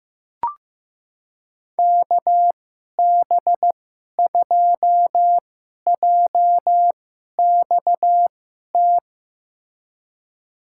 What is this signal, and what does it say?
Morse code, heard as a keyed tone, about 15 wpm, spaced as for 11 wpm, 700 Hz. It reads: KB2JXT
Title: USA Call Signs - 2x Character Spacing - 15wpm